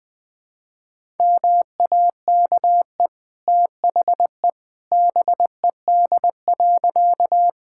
Morse code, heard as a keyed tone, about 20 wpm, 700 Hz. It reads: MAKE THE BED.
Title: Easy Sentences with Hints - 20wpm